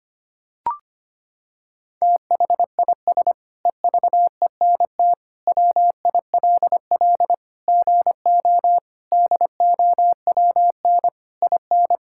Morse code, heard as a keyed tone, about 25 wpm, 700 Hz. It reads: THIS EVENT WILL GO DOWN IN
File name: practice / 25wpm / Sentences with 9-Word Phrases - 25wpm